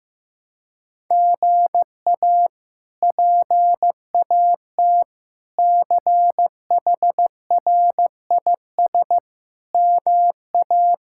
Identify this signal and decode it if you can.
Morse code, heard as a keyed tone, about 16 wpm, 700 Hz. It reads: GA PAT CHRIS MA
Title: K1USN SST QSO Practice - 15wpm